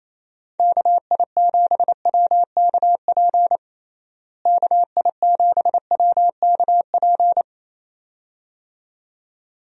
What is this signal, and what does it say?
Morse code, heard as a keyed tone, about 28 wpm, 700 Hz. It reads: KI7WKP KI7WKP
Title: USA Call Signs - 6 Characters - Repeated 2x - 28wpm